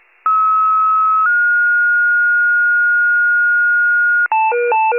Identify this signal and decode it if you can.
Two-tone page, heard as a one-second tone, then a three-second tone, about 1290 then 1430 Hz. It is heard from 0.3 to 4.3 s.
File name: wcalert